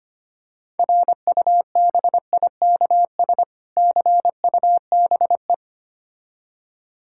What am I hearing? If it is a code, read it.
Morse code, heard as a keyed tone, about 25 wpm, 700 Hz. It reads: RUBIKS CUBE
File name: practice / 25wpm / Difficult Words with Hints - 25wpm